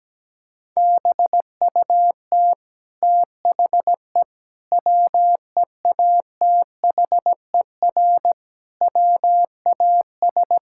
Morse code, heard as a keyed tone, about 17 wpm, 700 Hz. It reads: BUT THE WEATHER WAS
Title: Sentences from Top 1000 Words - Review Part 5 - 17wpm